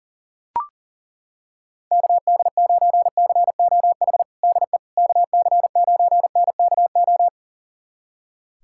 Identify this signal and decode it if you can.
Morse code, heard as a keyed tone, about 40 wpm, 700 Hz. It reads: KD9COH DE KC9NKO
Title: USA Callsign DE Callsign - 6 Characters - 40wpm